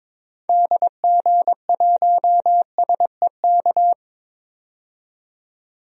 Morse code, heard as a keyed tone, about 22 wpm, 700 Hz. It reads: DG1SEK
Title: DXCC-100 Call Signs - 6 Characters - 22wpm